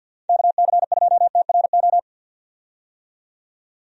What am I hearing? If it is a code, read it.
Morse code, heard as a keyed tone, about 49 wpm, 700 Hz. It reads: KC2TRO
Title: USA Call Signs - 6 Characters - 50wpm